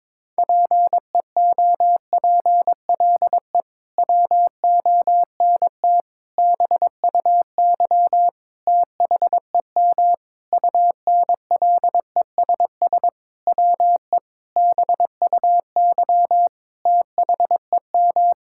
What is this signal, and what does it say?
Morse code, heard as a keyed tone, about 22 wpm, 700 Hz. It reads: PEOPLE WONT BUY THEM UNLESS WE BUY THEM
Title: Sentences from Top 1000 Words - Review Part 6 - 22wpm